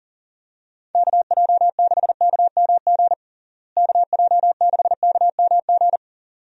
Morse code, heard as a keyed tone, about 40 wpm, 700 Hz. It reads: KJ6KMG KJ6KMG
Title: USA Call Signs - 6 Characters - Repeated 2x - 40wpm